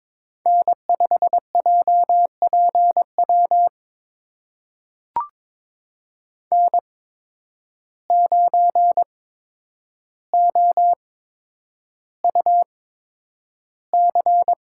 Morse code, heard as a keyed tone, about 22 wpm, spaced as for 9 wpm, 700 Hz. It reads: N5JPW N9OUC
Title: USA Call Signs - 5 Characters - 8x Character Spacing - 22wpm